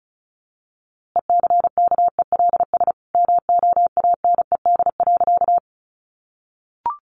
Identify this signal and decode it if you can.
Morse code, heard as a keyed tone, about 34 wpm, 700 Hz. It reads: ECKELS MOANED.
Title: A Sound Of Thunder 35wpm